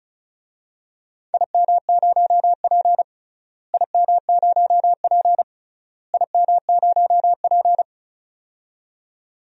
Morse code, heard as a keyed tone, about 36 wpm, 700 Hz. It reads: IM0P IM0P IM0P
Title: DXCC-100 Call Signs - 4 Characters - Repeated 3x - 35wpm